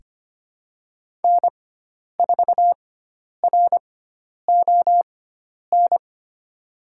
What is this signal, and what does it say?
Morse code, heard as a keyed tone, about 25 wpm, spaced as for 10 wpm, 700 Hz. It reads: N4RON
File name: Lesson 16g - Letter c - Call Signs 25wpm